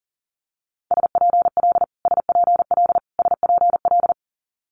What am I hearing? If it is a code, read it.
Morse code, heard as a keyed tone, about 40 wpm, 700 Hz. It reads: SPL SPL SPL
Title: Multi-letter Phonemes Repeated 3x 40wpm